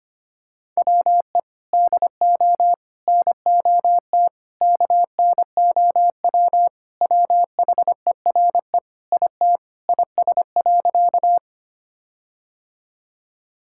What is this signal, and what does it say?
Morse code, heard as a keyed tone, about 25 wpm, 700 Hz. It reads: WE DO NOT KNOW WHERE IT IS.
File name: practice / 25wpm / Sentences with 4-Word Phrases - 25wpm